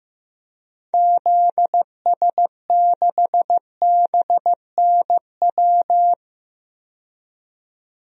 Morse code, heard as a keyed tone, about 15 wpm, 700 Hz. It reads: ZS6BNW
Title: DXCC-100 Call Signs - 6 Characters - 15wpm